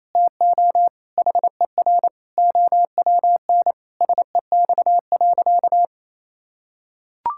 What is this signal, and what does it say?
Morse code, heard as a keyed tone, about 28 wpm, 700 Hz. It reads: TO HER OWN SEX.
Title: Sentences from Top 1600 Words 28wpm